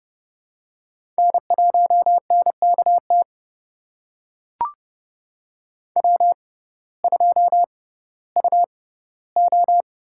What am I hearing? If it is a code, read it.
Morse code, heard as a keyed tone, about 32 wpm, spaced as for 17 wpm, 700 Hz. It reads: N1NKT W2UO
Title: USA Call Signs - 6x Character Spacing - 30wpm